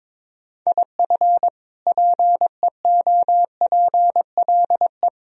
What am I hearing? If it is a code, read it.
Morse code, heard as a keyed tone, about 22 wpm, 700 Hz. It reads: IF PEOPLE
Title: Sentences from Top 1000 Words - Review Part 3 - 22wpm